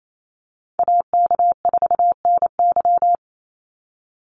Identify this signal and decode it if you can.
Morse code, heard as a keyed tone, about 28 wpm, 700 Hz. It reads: AK4NY
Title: Call Signs - 28wpm